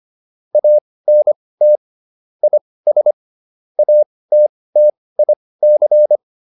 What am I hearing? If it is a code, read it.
Morse code, heard as a keyed tone, about 25 wpm, spaced as for 18 wpm, 600 Hz. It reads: ANT IS ATTIC